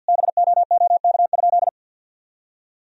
Morse code, heard as a keyed tone, about 50 wpm, 700 Hz. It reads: BOOK?
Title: Sentences from Top 1000 Words - Review Part 4 - 50wpm